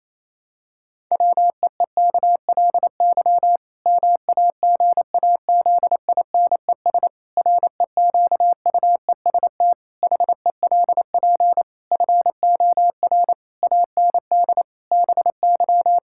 Morse code, heard as a keyed tone, about 28 wpm, 700 Hz. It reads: WEEKLY MAGAZINES REQUEST HELP FOR AND BY
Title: Pangram Challenge v2 - 28wpm